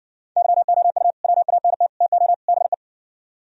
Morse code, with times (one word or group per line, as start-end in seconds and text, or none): 0.4-1.1 YOU
1.2-1.9 WANT
2.0-2.3 TO
2.5-2.7 BE